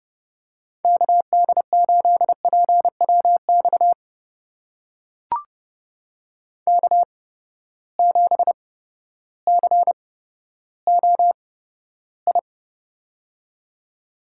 Morse code, heard as a keyed tone, about 30 wpm, spaced as for 14 wpm, 700 Hz. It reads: KD8PWX K7COI
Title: USA Call Signs - 8x Character Spacing - 30wpm